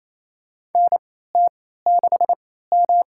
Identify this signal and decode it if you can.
Morse code, heard as a keyed tone, about 28 wpm, spaced as for 16 wpm, 700 Hz. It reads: NT6M
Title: USA Call Signs - 3x Character Spacing - 28wpm